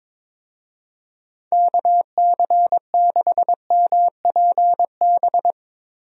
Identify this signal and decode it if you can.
Morse code, heard as a keyed tone, about 22 wpm, 700 Hz. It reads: KC6MPB
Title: USA Call Signs - 6 Characters - Repeated 3x - 22wpm